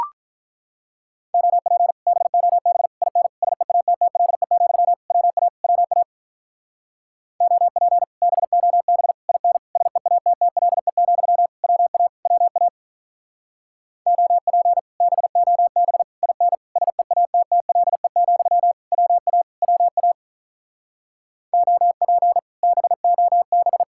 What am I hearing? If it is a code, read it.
Morse code, heard as a keyed tone, about 44 wpm, 700 Hz. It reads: OP BOB IN SEATTLE, WA WA OP BOB IN SEATTLE, WA WA OP BOB IN SEATTLE, WA WA OP BOB